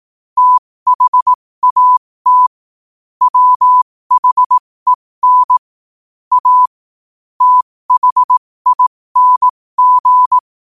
Morse code, heard as a keyed tone, about 18 wpm, spaced as for 15 wpm, 1000 Hz. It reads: THAT WHEN A THING
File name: qotdfiles / 15wpm